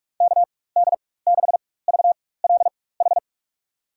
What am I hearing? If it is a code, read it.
Morse code, heard as a keyed tone, about 42 wpm, spaced as for 21 wpm, 700 Hz. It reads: KD6VLH
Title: USA Call Signs - 6 Characters - 4x Character Spacing - 45wpm